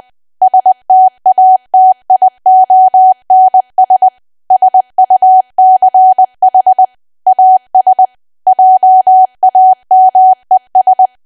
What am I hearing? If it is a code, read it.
Morse code, heard as a keyed tone, about 20 wpm, 750 Hz. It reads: STATIONS SUCH AS JAMES
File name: cwa / W1AW-Level3 / 20WPM